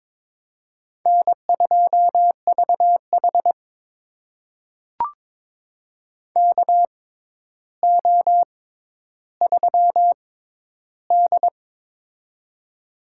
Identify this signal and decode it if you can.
Morse code, heard as a keyed tone, about 22 wpm, spaced as for 11 wpm, 700 Hz. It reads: N2VH KO3D
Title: USA Call Signs - 4 Characters - 6x Character Spacing - 22wpm